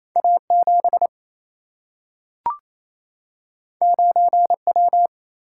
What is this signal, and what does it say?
Morse code, heard as a keyed tone, about 28 wpm, 700 Hz. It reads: A7 9W
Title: Single-Letter Single-Number 28wpm